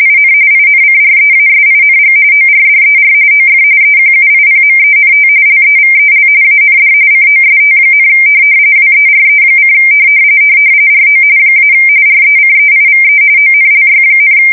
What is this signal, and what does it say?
RTTY, about 45.45 baud. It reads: IN INCREASING ORDER OF EQUIPMENT LEVEL AND COST, WERE: RO - RECEIVE ONLY,
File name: fsk170 / teletype-corp